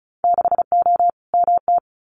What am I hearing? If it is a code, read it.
Morse code, heard as a keyed tone, about 34 wpm, 700 Hz. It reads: 6O MT
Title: Winter Field Day Exchanges Encore 35wpm